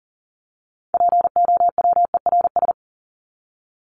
Morse code, heard as a keyed tone, about 40 wpm, 700 Hz. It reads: POWERS
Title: ES:S Suffix - Top 100 Words with Base Words 40wpm